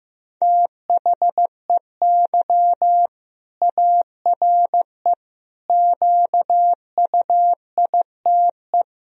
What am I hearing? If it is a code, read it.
Morse code, heard as a keyed tone, about 15 wpm, 700 Hz. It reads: THEY ARE QUITE
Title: Sentences from Top 1000 Words - Review Part 3 - 15wpm